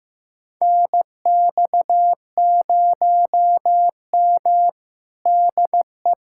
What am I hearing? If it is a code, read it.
Morse code, heard as a keyed tone, about 15 wpm, 700 Hz. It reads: NX0M DE
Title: USA Callsign DE Callsign - 4 Characters - 15wpm